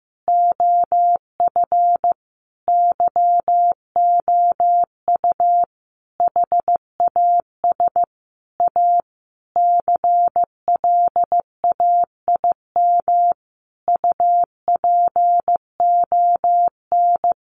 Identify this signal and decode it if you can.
Morse code, heard as a keyed tone, about 15 wpm, 700 Hz. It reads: OF YOU HAS A CLAIM UPON
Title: Sentences from Top 1500 Words 15wpm